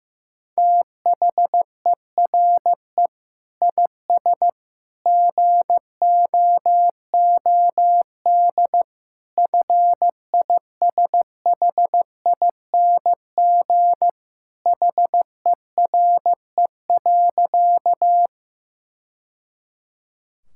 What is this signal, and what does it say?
Morse code, heard as a keyed tone, about 15 wpm, 700 Hz. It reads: THERE IS GOOD FISHING HERE.
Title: Sentences from Top 1000 Words - Review Part 5 - 15wpm